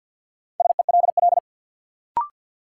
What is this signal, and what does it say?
Morse code, heard as a keyed tone, about 51 wpm, 700 Hz. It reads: SELL
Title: Sentences from Top 1100 Words - New Word Drill Alt - 50wpm